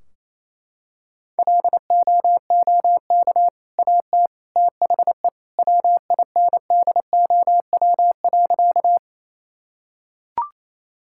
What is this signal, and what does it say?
Morse code, heard as a keyed tone, about 28 wpm, 700 Hz. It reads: LOOK AT THE WINDOW.
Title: Sentences from Top 1600 Words 28wpm